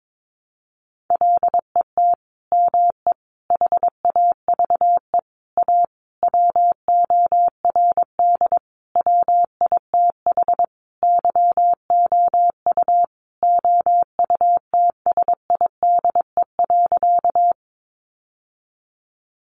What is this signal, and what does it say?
Morse code, heard as a keyed tone, about 22 wpm, 700 Hz. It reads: LET ME HAVE A WORD WITH YOU OUTSIDE.
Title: Sentences from Top 900 Words 22wpm